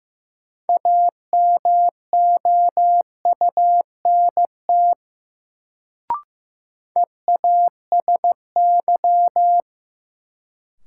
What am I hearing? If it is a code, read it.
Morse code, heard as a keyed tone, about 15 wpm, 700 Hz. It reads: AMOUNT EASY
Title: Sentences from Top 700 Words - New Word Drill - 15wpm